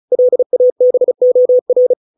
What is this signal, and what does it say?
Morse code, heard as a keyed tone, about 34 wpm, 500 Hz. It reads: LABOR
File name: 632 words / 632 Words 35WPM